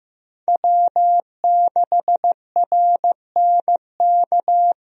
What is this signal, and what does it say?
Morse code, heard as a keyed tone, about 15 wpm, 700 Hz. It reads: W6RNK
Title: USA Call Signs - 5 Characters - Repeated 2x - 15wpm